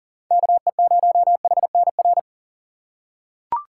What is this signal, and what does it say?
Morse code, heard as a keyed tone, about 37 wpm, 700 Hz. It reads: KE0HNR
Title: USA Call Signs - 6 Characters - 2x Character Spacing - 40wpm